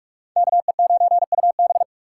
Morse code, heard as a keyed tone, about 42 wpm, 700 Hz. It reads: KE9UB
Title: USA Call Signs - 5 Characters - 2x Character Spacing - 45wpm